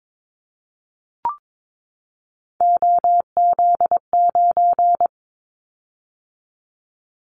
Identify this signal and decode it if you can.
Morse code, heard as a keyed tone, about 22 wpm, 700 Hz. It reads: OZ9
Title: Callsign Prefixes for Top 50 DXCC - 22wpm